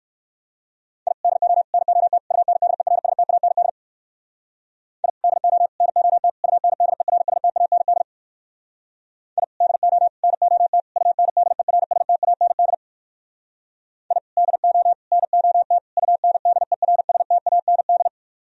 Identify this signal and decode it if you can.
Morse code, heard as a keyed tone, about 58 wpm, 700 Hz. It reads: I DO NOT UNDERSTAND I DO NOT UNDERSTAND I DO NOT UNDERSTAND I DO NOT UNDERSTAND